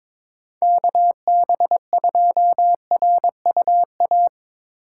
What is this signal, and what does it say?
Morse code, heard as a keyed tone, about 23 wpm, 700 Hz. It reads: KB2RUA
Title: USA Call Signs - 2x Character Spacing - 22wpm